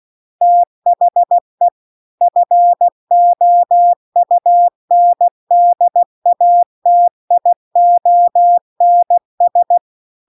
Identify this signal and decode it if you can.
Morse code, heard as a keyed tone, about 16 wpm, 700 Hz. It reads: THE FOUNDATIONS
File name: CW-The-Big-Twoheaded-River-1-16WPM